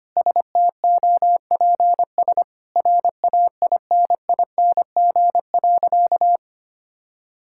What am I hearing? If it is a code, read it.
Morse code, heard as a keyed tone, about 25 wpm, 700 Hz. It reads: STOPS RAINING.
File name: practice / 25wpm / Sentences with 2-Word Phrases - 25wpm